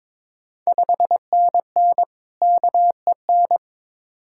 Morse code, heard as a keyed tone, about 22 wpm, 700 Hz. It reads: 5NN KEN
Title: Ontario QSO Party Exchange 22wpm